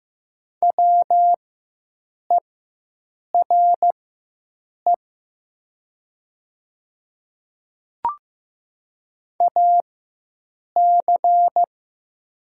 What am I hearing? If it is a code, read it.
Morse code, heard as a keyed tone, about 15 wpm, spaced as for 7 wpm, 700 Hz. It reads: WERE AC
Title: 4x Character Spacing - Top 200 Words - 15wpm